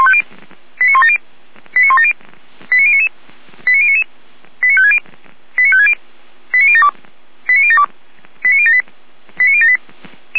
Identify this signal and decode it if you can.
Five-tone selective call (ZVEI), heard as quick runs of five tones, tones of about 70 ms. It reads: 89170 89170 89099 89099 89560 89560 89072 89072 89078 89078
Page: SDR/ZVEI